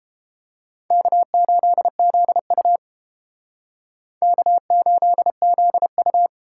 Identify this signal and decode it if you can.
Morse code, heard as a keyed tone, about 30 wpm, 700 Hz. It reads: K8ZU K8ZU